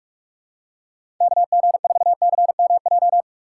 Morse code, heard as a keyed tone, about 42 wpm, 700 Hz. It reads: KG4CMJ
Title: USA Call Signs - 4x Character Spacing - 45wpm